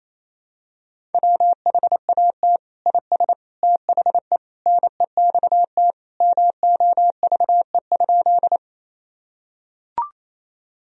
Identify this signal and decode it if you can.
Morse code, heard as a keyed tone, about 28 wpm, 700 Hz. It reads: WHAT IS THE NEXT MOVE?
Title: Sentences from Top 1000 Words - Review Part 1 - 28wpm